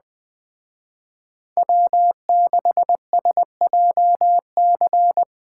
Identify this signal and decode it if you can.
Morse code, heard as a keyed tone, about 20 wpm, 700 Hz. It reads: W6SJC